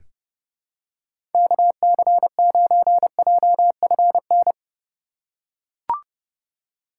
Morse code, heard as a keyed tone, about 32 wpm, 700 Hz. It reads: KC9JFN
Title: USA Call Signs - 6 Characters - 2x Character Spacing - 30wpm